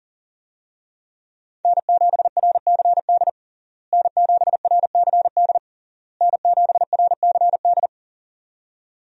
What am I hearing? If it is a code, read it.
Morse code, heard as a keyed tone, about 40 wpm, 700 Hz. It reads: N7RCD N7RCD N7RCD
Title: USA Call Signs - 5 Characters - Repeated 3x - 40wpm